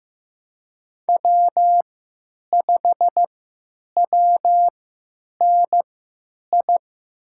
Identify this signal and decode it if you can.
Morse code, heard as a keyed tone, about 15 wpm, spaced as for 9 wpm, 700 Hz. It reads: W5WNI
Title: USA Call Signs - 5 Characters - 3x Character Spacing - 15wpm